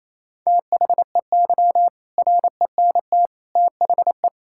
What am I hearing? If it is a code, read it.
Morse code, heard as a keyed tone, about 28 wpm, 700 Hz. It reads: THEY RENT THE
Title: Sentences from Top 1000 Words - Review Part 5 - 28wpm